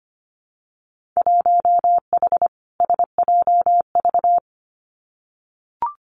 Morse code, heard as a keyed tone, about 25 wpm, 700 Hz. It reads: 1H SJV
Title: Winter Field Day Exchanges 25wpm